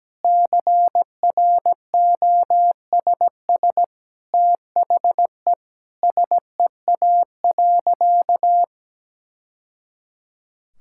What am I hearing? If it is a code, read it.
Morse code, heard as a keyed tone, about 17 wpm, 700 Hz. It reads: CROSS THE SEA.
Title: Sentences from Top 1000 Words - Review Part 6 - 17wpm